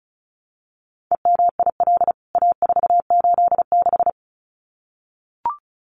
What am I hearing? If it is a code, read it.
Morse code, heard as a keyed tone, about 34 wpm, 700 Hz. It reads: EMIL A486
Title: CWT Exchange 35wpm